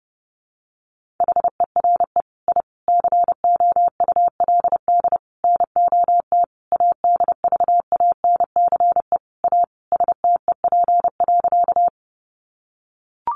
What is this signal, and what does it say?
Morse code, heard as a keyed tone, about 30 wpm, 700 Hz. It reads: HERE I COULD NOT ADVANCE A STEP.
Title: Sentences from Top 1400 Words 30wpm